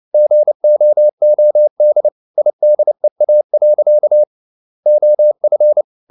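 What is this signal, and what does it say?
Morse code, heard as a keyed tone, about 29 wpm, 600 Hz. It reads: GOOD IDEA. OF
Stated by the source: Morse code at 29 wpm, sending Ken Follett's novel Never in English, from CW med Gnister.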